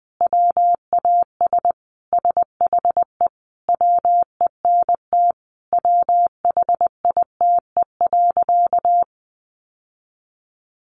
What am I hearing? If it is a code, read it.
Morse code, heard as a keyed tone, about 20 wpm, 700 Hz. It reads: WAS SHE WENT WHITE.
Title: Sentences from Top 1200 Words 20wpm